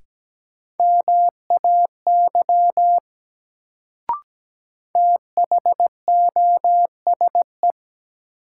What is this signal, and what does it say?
Morse code, heard as a keyed tone, about 17 wpm, 700 Hz. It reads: MAY THOSE
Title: Sets of 2 Words - New Word Drill - 17wpm